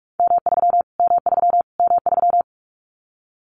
Morse code, heard as a keyed tone, about 44 wpm, 700 Hz. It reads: M3 M3 M3
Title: Callsign Prefixes for Top 25 DXCC - Repeated 3x - 45wpm